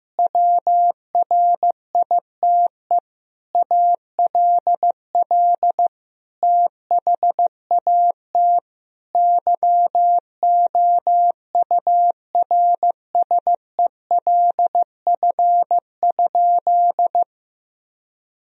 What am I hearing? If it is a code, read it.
Morse code, heard as a keyed tone, about 15 wpm, 700 Hz. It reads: WRITE ALL THAT YOURSELF?
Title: Sentences from Top 1000 Words - Review Part 1 - 15wpm